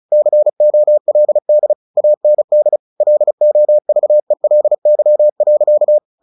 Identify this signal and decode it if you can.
Morse code, heard as a keyed tone, about 34 wpm, 600 Hz. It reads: COLD AND LOVELY.
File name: the old man and the sea / the old man and the sea 1 35wpm